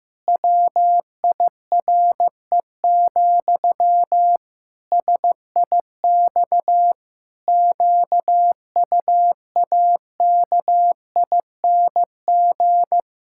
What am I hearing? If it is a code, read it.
Morse code, heard as a keyed tone, about 15 wpm, 700 Hz. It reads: WIRE, SIX QUAKING
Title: Pangram Challenge v2 - 15wpm